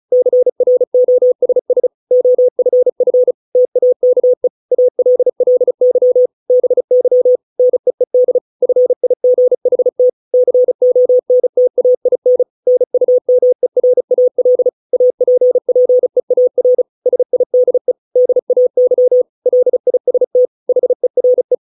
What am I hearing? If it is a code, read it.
Morse code, heard as a keyed tone, about 34 wpm, 500 Hz. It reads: CROSS OFF TAKE ALLY BY NEED FIGHT CONTAIN NUMERAL APPEAR SIDE DAY LIST HERE